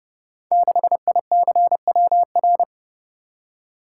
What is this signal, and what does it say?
Morse code, heard as a keyed tone, about 30 wpm, 700 Hz. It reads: 6ICWR
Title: Code Groups - 5 Letter:Numbers - 30wpm